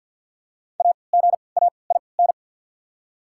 Morse code, heard as a keyed tone, about 54 wpm, spaced as for 29 wpm, 690 Hz. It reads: AGAIN
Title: 3x Character Spacing - Top 500 Words - 50wpm